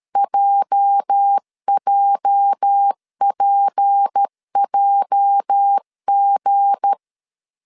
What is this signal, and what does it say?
Morse code, heard as a keyed tone, about 13 wpm, 800 Hz. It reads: JJPJG
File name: LEZIONI / 8 YJ45 / 8 YJ 12wpm 2235